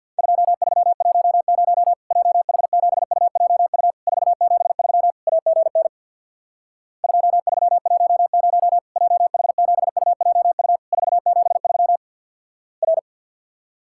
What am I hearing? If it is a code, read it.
Morse code, heard as a keyed tone, about 50 wpm, 700 Hz. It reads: 2310 JH7UJU 473 AGN 2310 JH7UJU 473 R
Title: Worked All Europe DX Contest - QTC Exchange with Repeats - 50wpm